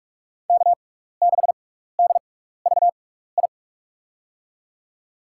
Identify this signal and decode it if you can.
Morse code, heard as a keyed tone, about 47 wpm, spaced as for 16 wpm, 700 Hz. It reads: K6DVI
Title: USA Call Signs - 5 Characters - 6x Character Spacing - 45wpm